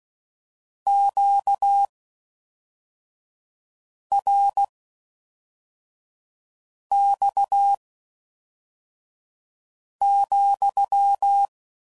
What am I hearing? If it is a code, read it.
Morse code, heard as a keyed tone, about 16 wpm, spaced as for 4 wpm, 800 Hz. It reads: QRX,